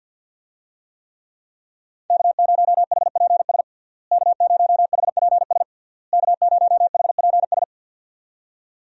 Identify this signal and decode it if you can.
Morse code, heard as a keyed tone, about 50 wpm, 700 Hz. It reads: K0HPS K0HPS K0HPS
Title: USA Call Signs - 5 Characters - Repeated 3x - 50wpm